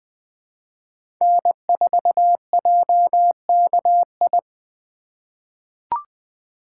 Morse code, heard as a keyed tone, about 20 wpm, 700 Hz. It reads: N4JKI
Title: USA Call Signs - 5 Characters - 3x Character Spacing - 20wpm